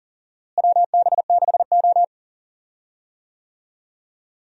Morse code, heard as a keyed tone, about 37 wpm, 700 Hz. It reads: WB6O
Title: USA Call Signs - 4 Characters - Repeated 2x - 40wpm